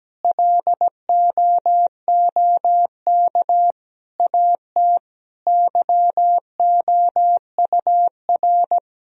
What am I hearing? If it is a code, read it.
Morse code, heard as a keyed tone, about 17 wpm, 700 Hz. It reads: LOOK AT YOUR
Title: Sentences from Top 1000 Words - Review Part 1 - 17wpm